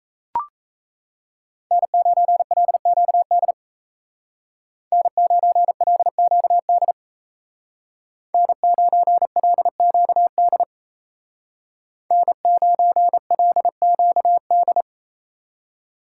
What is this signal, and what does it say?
Morse code, heard as a keyed tone, about 33 wpm, 700 Hz. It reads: N9LQD N9LQD N9LQD N9LQD
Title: Speed Racing - USA 1x3 Call Signs 28wpm